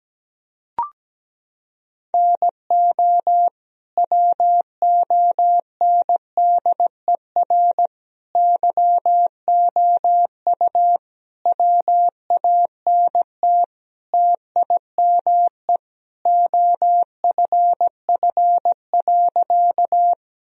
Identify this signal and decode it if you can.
Morse code, heard as a keyed tone, about 17 wpm, 700 Hz. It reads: NO WONDER YOU WANT TIME OFF.
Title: Sentences from Top 1000 Words - Review Part 3 - 17wpm